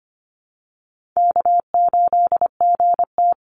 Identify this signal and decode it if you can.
Morse code, heard as a keyed tone, about 25 wpm, 700 Hz. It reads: K8GT
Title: Easy Call Signs Repeated 2x - 25wpm